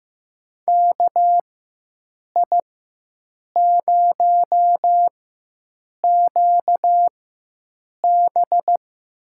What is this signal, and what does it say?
Morse code, heard as a keyed tone, about 15 wpm, spaced as for 7 wpm, 700 Hz. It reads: KI0QB
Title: USA Call Signs - 5 Characters - 4x Character Spacing - 15wpm